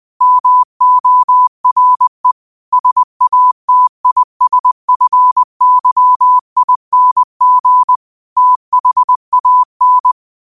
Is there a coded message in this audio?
Yes: Morse code from 0.2 to 10.1 s, heard as a keyed tone, about 20 wpm, 1000 Hz. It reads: MORE SATISFYING THAN